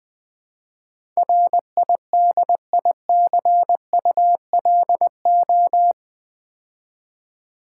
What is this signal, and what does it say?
Morse code, heard as a keyed tone, about 21 wpm, 700 Hz. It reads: RIDICULO
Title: Build A Word - 4 Letters At A Time - 10-Letter Words - 20wpm